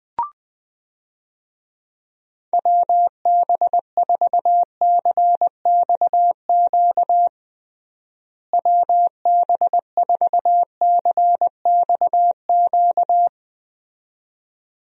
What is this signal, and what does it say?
Morse code, heard as a keyed tone, about 20 wpm, 700 Hz. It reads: WB4CXQ WB4CXQ
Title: USA Call Signs - 6 Characters - Repeated 2x - 20wpm